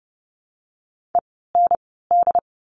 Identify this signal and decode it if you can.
Morse code, heard as a keyed tone, about 30 wpm, spaced as for 17 wpm, 700 Hz. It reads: END
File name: Lesson 17f - Letter u - Words 30wpm